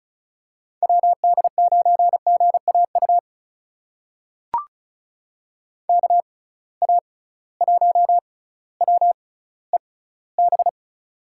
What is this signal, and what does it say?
Morse code, heard as a keyed tone, about 36 wpm, spaced as for 18 wpm, 700 Hz. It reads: WD9GAU KA1WEB
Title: USA Call Signs - 6 Characters - 6x Character Spacing - 35wpm